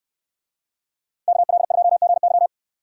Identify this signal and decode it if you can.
Morse code, heard as a keyed tone, about 69 wpm, 700 Hz. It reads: KL2GY